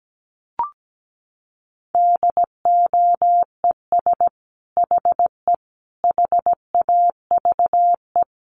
Morse code, heard as a keyed tone, about 17 wpm, 700 Hz. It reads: DOES HE HAVE
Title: Sentences from Top 1300 Words 17wpm